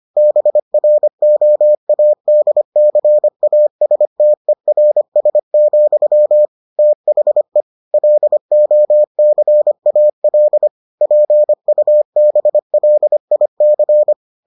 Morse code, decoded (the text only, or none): BROADCASTERS, THE LOCAL PUBLIC